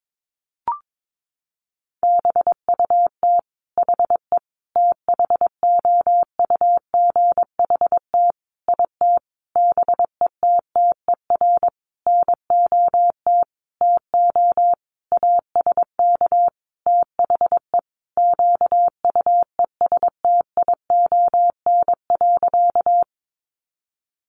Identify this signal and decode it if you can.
Morse code, heard as a keyed tone, about 22 wpm, 700 Hz. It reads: BUT HE THOUGHT IT BETTER NOT TO ASK THE QUESTION.
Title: Sentences from Top 600 Words 22wpm